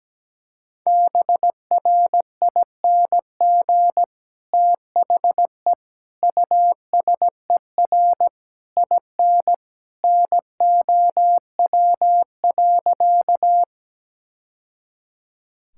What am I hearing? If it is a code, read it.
Morse code, heard as a keyed tone, about 17 wpm, 700 Hz. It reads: BRING THE USER IN NOW.
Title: Sentences from Top 1000 Words - Review Part 5 - 17wpm